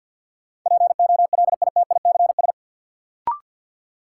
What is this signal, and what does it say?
Morse code, heard as a keyed tone, about 52 wpm, 700 Hz. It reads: POLITICS
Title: Sentences from Top 1000 Words - New Word Drill - 50wpm